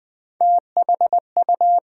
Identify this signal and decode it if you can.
Morse code, heard as a keyed tone, about 20 wpm, 700 Hz. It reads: THU
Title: Build A Word - 1 Letter At A Time - 4-Letter Words - 20wpm